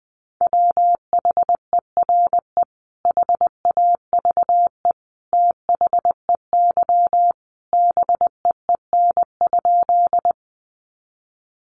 Morse code, heard as a keyed tone, about 20 wpm, 700 Hz. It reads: WHERE HAVE THEY BEEN?
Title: Sentences from Top 200 Words 20wpm